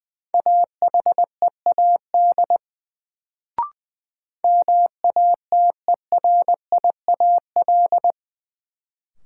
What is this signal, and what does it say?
Morse code, heard as a keyed tone, about 20 wpm, 700 Hz. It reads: AHEAD MATERIAL
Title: Sentences from Top 1100 Words - New Word Drill Alt - 20wpm